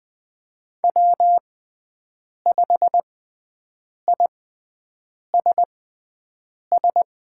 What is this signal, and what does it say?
Morse code, heard as a keyed tone, about 20 wpm, spaced as for 7 wpm, 700 Hz. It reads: W5ISS
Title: USA Call Signs - 5 Characters - 6x Character Spacing - 20wpm